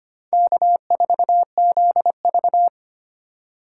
Morse code, heard as a keyed tone, about 32 wpm, 700 Hz. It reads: K4ZV